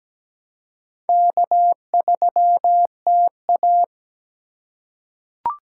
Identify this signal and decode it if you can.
Morse code, heard as a keyed tone, about 17 wpm, 700 Hz. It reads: K3TA
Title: USA Call Signs - 2x Character Spacing - 17wpm